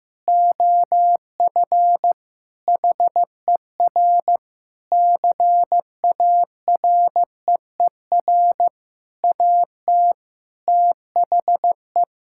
Morse code, heard as a keyed tone, about 15 wpm, 700 Hz. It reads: OF HER CAREER AT THE
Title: Sentences from Top 1000 Words - Review Part 6 - 15wpm